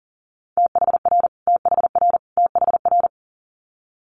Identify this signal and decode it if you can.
Morse code, heard as a keyed tone, about 38 wpm, 700 Hz. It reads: THR THR THR
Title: Multi-letter Phonemes Repeated 3x 40wpm